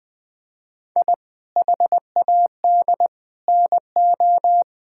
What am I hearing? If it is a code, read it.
Morse code, heard as a keyed tone, about 21 wpm, 700 Hz. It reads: I HAD NO
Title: Sentences with 7-Word Phrases - 20wpm